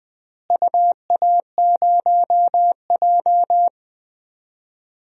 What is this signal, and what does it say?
Morse code, heard as a keyed tone, about 20 wpm, 700 Hz. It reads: UA0J
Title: DXCC-100 Call Signs - 4 Characters - Repeated 3x - 20wpm